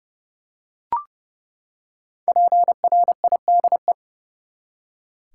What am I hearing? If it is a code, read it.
Morse code, heard as a keyed tone, about 30 wpm, 700 Hz. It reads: PRIDE
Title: Five Letter Words - 30wpm